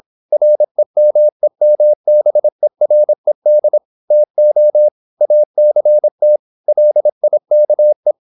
Morse code, heard as a keyed tone, about 26 wpm, 600 Hz. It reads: REMEMBERED TO ACT LIKE
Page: Never 26wpm | CW med Gnister